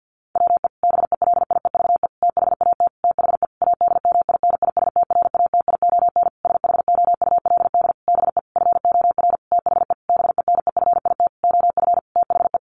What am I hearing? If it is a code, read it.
Morse code, heard as a keyed tone, about 50 wpm, 700 Hz. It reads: WE BELIEVE THAT THE ADMINISTRATION SHOULD BE FOR THE BENEFIT OF THE
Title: Sentences from Top 1300 Words 50wpm